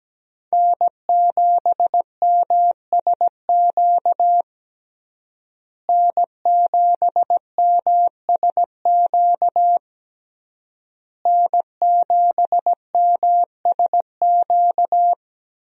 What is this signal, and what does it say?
Morse code, heard as a keyed tone, about 17 wpm, 700 Hz. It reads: N7MSQ N7MSQ N7MSQ
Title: USA Call Signs - 5 Characters - Repeated 3x - 17wpm